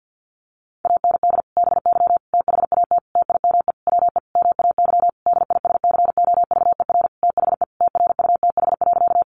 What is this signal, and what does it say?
Morse code, heard as a keyed tone, about 50 wpm, 700 Hz. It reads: AND BY THAT TIME WE MAY DISCOVER THE TRUTH.
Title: Sentences from Top 1100 Words 50wpm